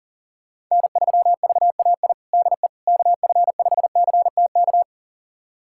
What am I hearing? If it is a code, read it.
Morse code, heard as a keyed tone, about 40 wpm, 700 Hz. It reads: N3VAI DE KF5CTK
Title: USA Callsign DE Callsign 40wpm